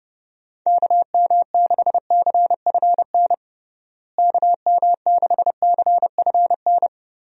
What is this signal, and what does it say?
Morse code, heard as a keyed tone, about 30 wpm, 700 Hz. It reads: KM6CFN KM6CFN
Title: USA Call Signs - 6 Characters - Repeated 2x - 30wpm